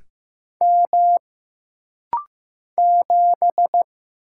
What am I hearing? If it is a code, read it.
Morse code, heard as a keyed tone, about 15 wpm, 700 Hz. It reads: M 7
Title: Single Letter-Number Alt - Rapid-Fire 15wpm